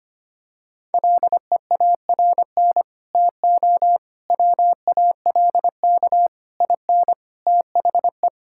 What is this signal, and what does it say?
Morse code, heard as a keyed tone, about 25 wpm, 700 Hz. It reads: LEARN TO WALK IN THE
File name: Sets of 9 Words - 25wpm